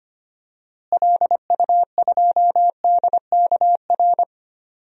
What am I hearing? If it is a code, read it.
Morse code, heard as a keyed tone, about 25 wpm, 700 Hz. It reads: LU2DKR
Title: DXCC-100 Call Signs - 6 Characters - Repeated 2x - 25wpm